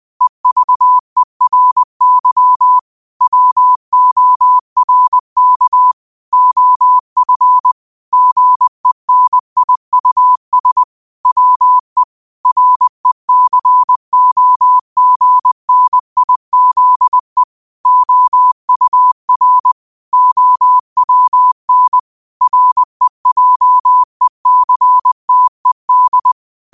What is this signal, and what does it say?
Morse code, heard as a keyed tone, about 20 wpm, 1000 Hz. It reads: EVERY WORK OF GENIUS WE RECOGNIZE OUR OWN REJECTED